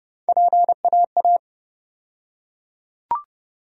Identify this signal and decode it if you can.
Morse code, heard as a keyed tone, about 30 wpm, 700 Hz. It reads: PAA
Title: USA Call Sign Postfixes - 3 Letters 30wpm